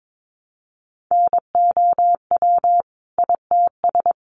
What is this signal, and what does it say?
Morse code, heard as a keyed tone, about 22 wpm, 700 Hz. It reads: NOW ITS
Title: Sentences from Top 600 Words 22wpm